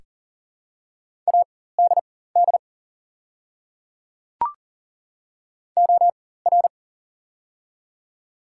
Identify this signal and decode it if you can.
Morse code, heard as a keyed tone, about 40 wpm, spaced as for 19 wpm, 700 Hz. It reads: ADD OR
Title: 4x Character Spacing - Top 500 Words - 40wpm